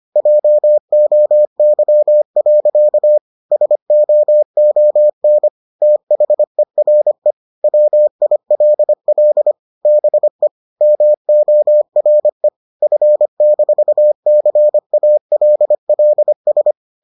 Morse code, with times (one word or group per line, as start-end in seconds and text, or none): 0.2-3.2 JOY.
3.5-5.5 SOON
5.8-7.3 THERE
7.6-9.5 WILL
9.9-10.5 BE
10.8-12.5 MORE
12.8-16.7 F-CALLS